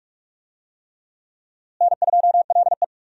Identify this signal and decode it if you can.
Morse code, heard as a keyed tone, about 47 wpm, 700 Hz. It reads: N2LE
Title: USA Call Signs - 4 Characters - Repeated 3x - 45wpm